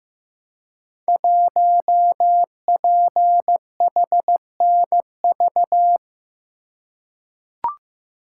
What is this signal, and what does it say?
Morse code, heard as a keyed tone, about 15 wpm, 700 Hz. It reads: 1PHNV
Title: Code Groups - 5 Letter:Numbers - 15wpm